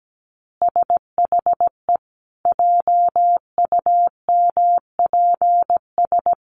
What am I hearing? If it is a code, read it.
Morse code, heard as a keyed tone, about 17 wpm, 700 Hz. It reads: SHE JUMPS
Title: Easy Phrases 17wpm